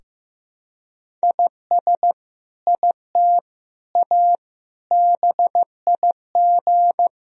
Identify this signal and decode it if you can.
Morse code, heard as a keyed tone, about 16 wpm, 700 Hz. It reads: IS IT A BIG
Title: Sentences from Top 1000 Words - Review Part 3 - 15wpm